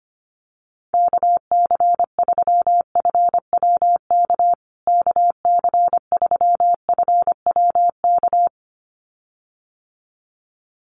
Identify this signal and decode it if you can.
Morse code, heard as a keyed tone, about 25 wpm, 700 Hz. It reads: KC3FWK KC3FWK
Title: USA Call Signs Repeated 2x 25wpm